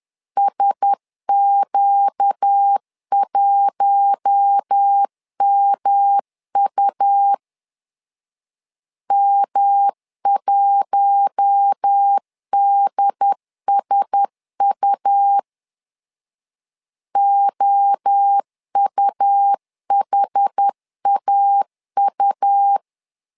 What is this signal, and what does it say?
Morse code, heard as a keyed tone, about 10 wpm, 800 Hz. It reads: SQ1MU M1DSU OUHAU
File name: LEZIONI / 4 Q1S 10wpm 2133